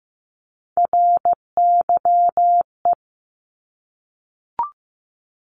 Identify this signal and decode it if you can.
Morse code, heard as a keyed tone, about 15 wpm, 700 Hz. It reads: RYE
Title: Multi-letter Phonemes 15wpm